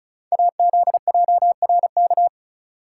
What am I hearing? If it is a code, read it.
Morse code, heard as a keyed tone, about 36 wpm, 690 Hz. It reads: AZJRK
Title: Code Groups - 5 Letter:Numbers - 35wpm